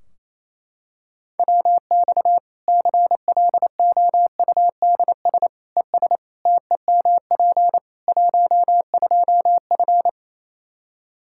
Morse code, heard as a keyed tone, about 28 wpm, 700 Hz. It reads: WX CLOUDS ES TEMP 12F
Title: Practice QSO - Exchange 2 - Info - 28wpm